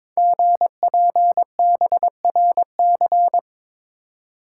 Morse code, heard as a keyed tone, about 22 wpm, 700 Hz. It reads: GPBRC
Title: Code Groups - 5 Letters - 22wpm